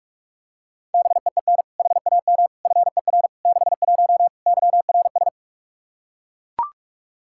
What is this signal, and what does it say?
Morse code, heard as a keyed tone, about 41 wpm, 700 Hz. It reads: BEEN HAM FER 61 YRS
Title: Practice QSO - Exchange 3 - More Info - 45wpm